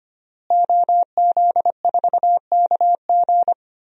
Morse code, heard as a keyed tone, about 25 wpm, 700 Hz. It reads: OZ4KG
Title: DXCC-100 Call Signs - 5 Characters - Repeated 3x - 25wpm